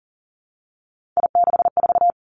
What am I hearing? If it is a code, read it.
Morse code, heard as a keyed tone, about 40 wpm, 700 Hz. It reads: I64
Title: Three Challenging Letter-Number Combinations 40wpm